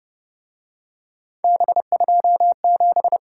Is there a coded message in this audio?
Yes: Morse code from 1.4 to 3.2 s, heard as a keyed tone, about 29 wpm, 700 Hz. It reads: B27